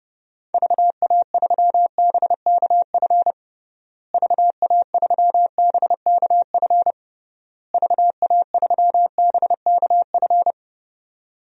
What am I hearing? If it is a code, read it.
Morse code, heard as a keyed tone, about 30 wpm, 700 Hz. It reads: VA3BKF VA3BKF VA3BKF
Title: DXCC-100 Call Signs - 6 Characters - Repeated 3x - 30wpm